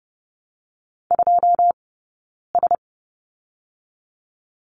Morse code, heard as a keyed tone, about 30 wpm, 700 Hz. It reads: 2 S
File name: course / 30:10wpm / 099 Lesson 17d - Letter u - Two Characters 30wpm